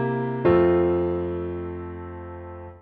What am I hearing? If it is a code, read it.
Dial tone (North America), 350 + 440 Hz.